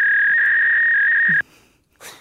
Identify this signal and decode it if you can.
Fax machine handshake.